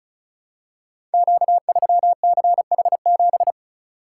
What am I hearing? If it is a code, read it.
Morse code, heard as a keyed tone, about 36 wpm, 690 Hz. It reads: Q3CH7
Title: Code Groups - 5 Letter:Numbers - 35wpm